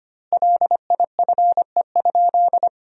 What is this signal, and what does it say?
Morse code, heard as a keyed tone, about 25 wpm, 700 Hz. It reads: LIFE?
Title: 4x Word-Spacing - Sentences from Top 300 Words - 25wpm